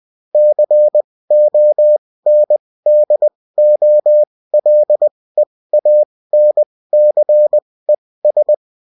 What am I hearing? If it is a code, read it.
Morse code, heard as a keyed tone, about 20 wpm, spaced as for 16 wpm, 600 Hz. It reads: CONDOLEANCES